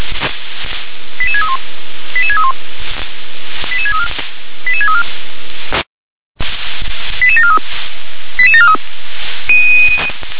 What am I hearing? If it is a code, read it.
Five-tone selective call (ZVEI), heard as quick runs of five tones, tones of about 70 ms. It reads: 88631 88631 88634 88634 88743 88743